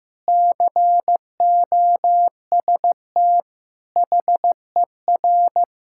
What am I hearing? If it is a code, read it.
Morse code, heard as a keyed tone, about 14 wpm, 700 Hz. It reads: COST HER
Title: Sentences from Top 1600 Words 15wpm